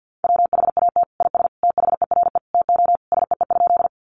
Morse code, heard as a keyed tone, about 50 wpm, 700 Hz. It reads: WHAT IS THERE TO SEE?
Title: Sentences from Top 100 Words 50wpm